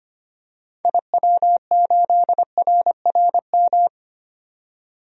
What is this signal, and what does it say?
Morse code, heard as a keyed tone, about 25 wpm, 700 Hz. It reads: IW8RRM
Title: DXCC-100 Call Signs - 6 Characters - 25wpm